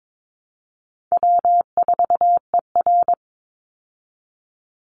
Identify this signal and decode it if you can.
Morse code, heard as a keyed tone, about 22 wpm, 700 Hz. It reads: W4ER
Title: Easy Call Signs Repeated 3x - 22wpm